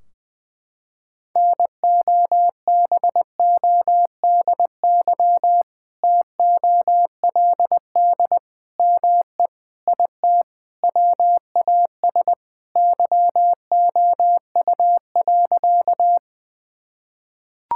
Morse code, heard as a keyed tone, about 19 wpm, 700 Hz. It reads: NOBODY TOLD ME IT WAS YOU.
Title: Sentences from Top 1600 Words 20wpm